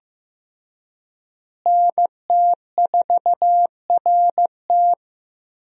Morse code, heard as a keyed tone, about 15 wpm, 700 Hz. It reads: NT4RT
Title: USA Call Signs - 5 Characters - Repeated 3x - 15wpm